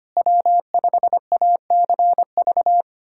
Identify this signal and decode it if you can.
Morse code, heard as a keyed tone, about 25 wpm, 700 Hz. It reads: W5ACV
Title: USA Callsign DE Callsign - 5 Characters - 25wpm